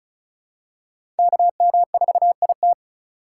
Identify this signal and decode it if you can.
Morse code, heard as a keyed tone, about 34 wpm, 700 Hz. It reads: KM4IT
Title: USA Call Signs - 3x Character Spacing - 35wpm